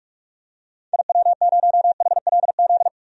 Morse code, heard as a keyed tone, about 46 wpm, 700 Hz. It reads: IW0HLZ
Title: DXCC-100 Call Signs - 6 Characters - 45wpm